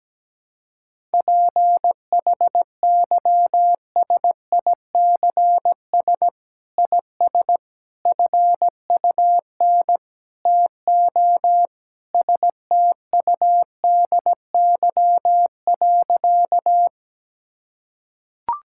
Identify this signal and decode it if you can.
Morse code, heard as a keyed tone, about 17 wpm, 700 Hz. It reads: PHYSICS IS FUN TO STUDY.
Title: Sentences from Top 1000 Words - Review Part 5 - 17wpm